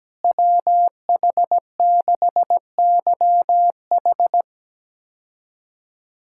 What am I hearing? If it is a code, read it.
Morse code, heard as a keyed tone, about 17 wpm, 700 Hz. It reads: WH6YH
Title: K1USN SST QSO Practice - 17wpm